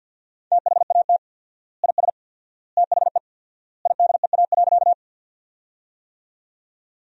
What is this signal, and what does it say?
Morse code, heard as a keyed tone, about 53 wpm, 700 Hz. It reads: THAT IS THE IDEA.
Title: 4x Word-Spacing - Sentences from Top 300 Words - 50wpm